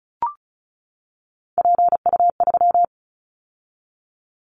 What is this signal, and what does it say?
Morse code, heard as a keyed tone, about 36 wpm, 700 Hz. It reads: PU3
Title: Callsign Prefixes for Top 100 DXCC - 35wpm